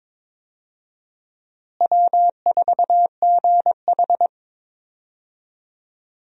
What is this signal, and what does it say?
Morse code, heard as a keyed tone, about 22 wpm, 700 Hz. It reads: W4GH
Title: USA Call Signs - 4 Characters - Repeated 3x - 22wpm